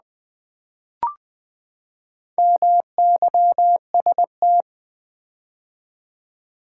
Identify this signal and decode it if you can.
Morse code, heard as a keyed tone, about 20 wpm, 700 Hz. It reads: MYST
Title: Build A Word - 4 Letters At A Time - 10-Letter Words - 20wpm